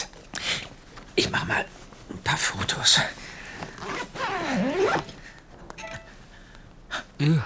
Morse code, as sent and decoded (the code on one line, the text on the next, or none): none